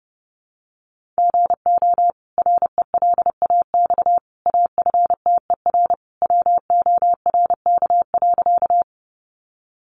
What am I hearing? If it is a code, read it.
Morse code, heard as a keyed tone, about 28 wpm, 700 Hz. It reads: GO RELAX AFTER WORK.
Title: Sentences from Top 1400 Words 30wpm